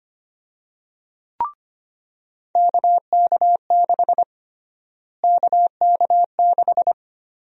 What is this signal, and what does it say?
Morse code, heard as a keyed tone, about 25 wpm, 700 Hz. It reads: KK6 KK6
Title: USA Call Sign Prefixes - Repeated 2x - 25wpm